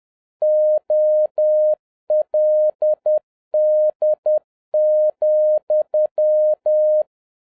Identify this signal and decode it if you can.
Morse code, heard as a keyed tone, about 10 wpm, 610 Hz. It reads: OLD,
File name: 2011-10-mp3 / n11-MEETING WITH GOOD FRIENDS 10wpm